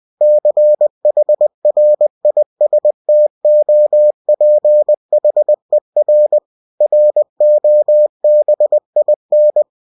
Morse code, heard as a keyed tone, about 20 wpm, 600 Hz. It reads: CHRISTOPHER ROBIN
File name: cw-books / winnie-the-pooh / winnie-20wpm-page-0033